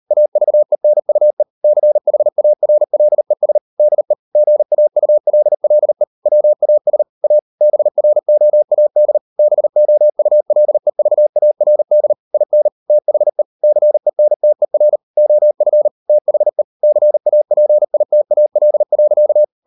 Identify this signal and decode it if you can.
Morse code, heard as a keyed tone, about 38 wpm, 600 Hz. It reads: AVENUE CHARLES DE GAULLE WAS A BROAD BOULEVARD IN THE CENTER OF THE CAPITAL.